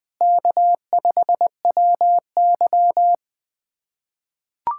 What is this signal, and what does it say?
Morse code, heard as a keyed tone, about 20 wpm, 700 Hz. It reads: K5WY
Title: USA Callsign DE Callsign - 4 Characters - 20wpm